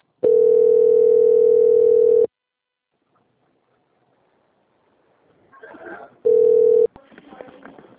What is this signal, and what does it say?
Ringback tone (North America), 440 + 480 Hz.